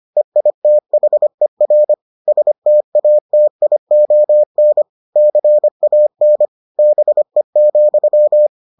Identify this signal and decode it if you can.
Morse code, heard as a keyed tone, about 25 wpm, 600 Hz. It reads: EITHER STATION CAN BE,